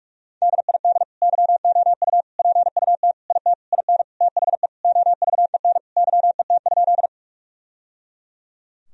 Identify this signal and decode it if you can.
Morse code, heard as a keyed tone, about 41 wpm, 700 Hz. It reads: DID YOU PUT IT IN THE OVEN YET?
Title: Sentences from Top 1000 Words - Review Part 4 - 45wpm